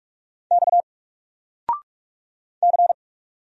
Morse code, heard as a keyed tone, about 46 wpm, 700 Hz. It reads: X C
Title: Single Letter-Number 45wpm